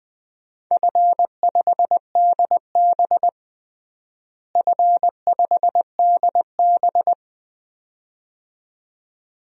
Morse code, heard as a keyed tone, about 20 wpm, 700 Hz. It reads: F5DB F5DB
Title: DXCC-100 Call Signs - 4 Characters - Repeated 2x - 20wpm